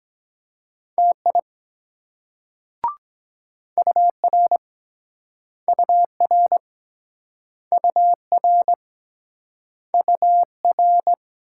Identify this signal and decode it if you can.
Morse code, heard as a keyed tone, about 24 wpm, 700 Hz. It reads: TI UR UR UR UR